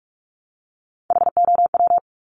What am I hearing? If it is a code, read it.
Morse code, heard as a keyed tone, about 46 wpm, 700 Hz. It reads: HOW
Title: Multi-letter Phonemes Repeated 3x 45wpm